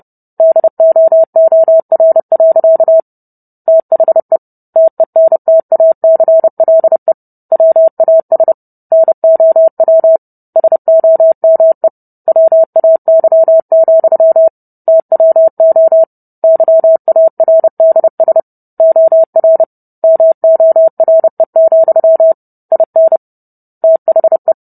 Morse code, heard as a keyed tone, about 30 wpm, 650 Hz. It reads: DOOR. THE TENTACLE WAS NOW SOME WAY, TWO YARDS OR MORE, IN THE